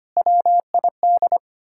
Morse code, heard as a keyed tone, about 33 wpm, 700 Hz. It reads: WID